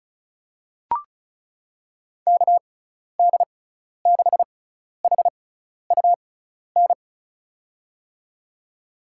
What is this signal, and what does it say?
Morse code, heard as a keyed tone, about 34 wpm, spaced as for 12 wpm, 700 Hz. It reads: KD6HUN
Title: USA Call Signs - 6x Character Spacing - 35wpm